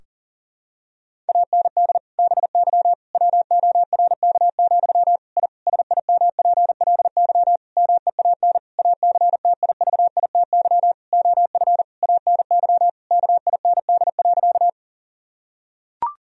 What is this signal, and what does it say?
Morse code, heard as a keyed tone, about 40 wpm, 700 Hz. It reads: AND BY WORK, I SIMPLY MEAN ACTIVITY OF ANY KIND.
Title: Sentences from Top 1000 Words - Review Part 4 - 40wpm